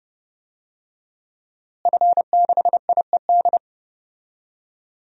Morse code, heard as a keyed tone, about 30 wpm, 700 Hz. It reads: F6IED
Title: DXCC-100 Call Signs - 5 Characters - Repeated 2x - 30wpm